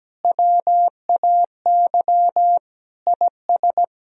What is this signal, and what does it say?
Morse code, heard as a keyed tone, about 17 wpm, 700 Hz. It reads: WAY IS
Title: Sentences from Top 1000 Words - Review Part 3 - 17wpm